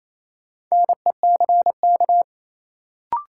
Morse code, heard as a keyed tone, about 28 wpm, 700 Hz. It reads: NECK
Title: Sets of 5 Words Encore - New Word Drill - 28wpm